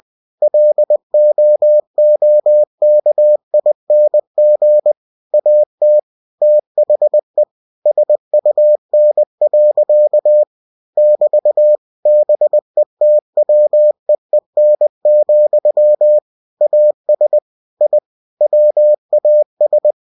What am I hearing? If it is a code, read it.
Morse code, heard as a keyed tone, about 20 wpm, 600 Hz. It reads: LOOKING AT THE SUN. <BT> BETWEEN, AS I WAS
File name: cw-books / winnie-the-pooh / winnie-20wpm-page-0024